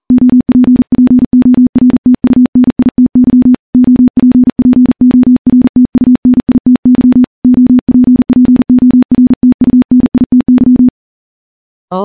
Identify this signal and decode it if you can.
Morse code, heard as a keyed tone, about 40 wpm, 250 Hz. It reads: OPPORTUNITY OPPORTUNITY OPPORTUNITY
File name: morse-training-40WPM